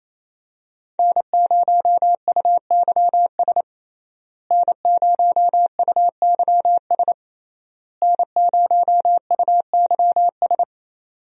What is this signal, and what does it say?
Morse code, heard as a keyed tone, about 28 wpm, 700 Hz. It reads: N0UYS N0UYS N0UYS
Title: USA Call Signs - 5 Characters - Repeated 3x - 28wpm